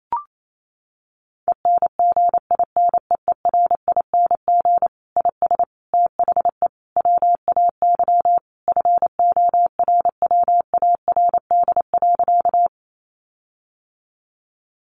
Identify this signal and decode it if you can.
Morse code, heard as a keyed tone, about 28 wpm, 700 Hz. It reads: ENGINEERING IS THE WAY FORWARD.
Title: Sentences from Top 1500 Words 28wpm